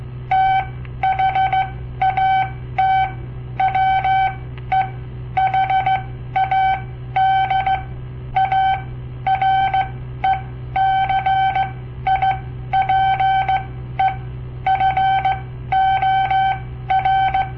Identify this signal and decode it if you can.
Morse code, heard as a keyed tone, about 16 wpm, spaced as for 12 wpm, 770 Hz. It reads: THAT WE HAD A RECIPE FOR
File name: papertapes / local / Code Practice-01